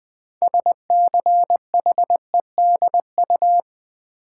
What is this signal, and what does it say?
Morse code, heard as a keyed tone, about 20 wpm, 700 Hz. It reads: SCHEDU
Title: Build A Word - 2 Letters At A Time - 8-Letter Words - 20wpm